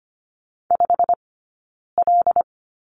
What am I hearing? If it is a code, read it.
Morse code, heard as a keyed tone, about 25 wpm, 700 Hz. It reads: 5 L
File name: Lesson 15d - Number 5 - Two Characters 25wpm